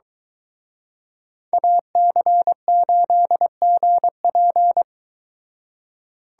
Morse code, heard as a keyed tone, about 23 wpm, 700 Hz. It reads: AC8GP